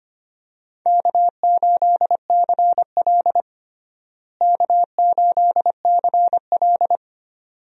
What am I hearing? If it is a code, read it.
Morse code, heard as a keyed tone, about 25 wpm, 700 Hz. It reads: K8CL K8CL
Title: USA Call Signs - 4 Characters - Repeated 2x - 25wpm